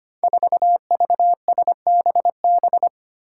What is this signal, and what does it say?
Morse code, heard as a keyed tone, about 25 wpm, 700 Hz. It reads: 4VSBB
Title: Five Challenging Letter-Number Combinations 25wpm